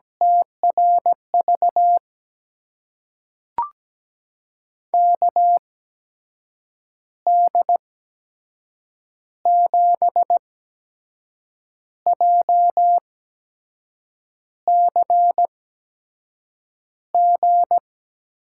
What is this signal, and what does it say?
Morse code, heard as a keyed tone, about 17 wpm, spaced as for 6 wpm, 700 Hz. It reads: TRV KD7JCG
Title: USA Call Signs - 6 Characters - 8x Character Spacing - 17wpm